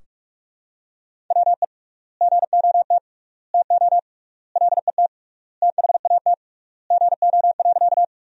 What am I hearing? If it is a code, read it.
Morse code, heard as a keyed tone, about 46 wpm, 700 Hz. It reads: WE GOT TO LET THAT GO.
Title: 3x Word-Spacing - Sentences from Top 300 Words - 45wpm